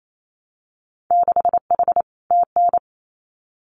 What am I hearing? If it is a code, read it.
Morse code, heard as a keyed tone, about 28 wpm, 700 Hz. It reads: 6H TN
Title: Winter Field Day Exchanges Encore 28wpm